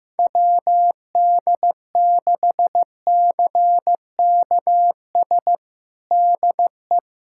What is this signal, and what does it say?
Morse code, heard as a keyed tone, about 16 wpm, 700 Hz. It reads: WD6CKS DE
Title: USA Callsign DE Callsign - 6 Characters - 15wpm